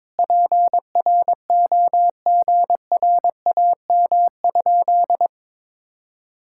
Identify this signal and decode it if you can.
Morse code, heard as a keyed tone, about 22 wpm, 700 Hz. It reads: PROGRAM?
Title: Sentences from Top 1000 Words - Review Part 5 - 22wpm